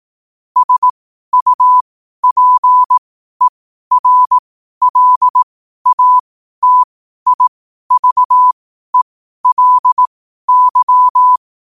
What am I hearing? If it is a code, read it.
Morse code, heard as a keyed tone, about 18 wpm, spaced as for 12 wpm, 1000 Hz. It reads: SUPERLATIVELY
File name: qotdfiles / 12wpm